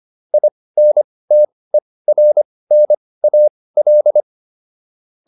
Morse code, heard as a keyed tone, about 25 wpm, spaced as for 18 wpm, 600 Hz. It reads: INTERNAL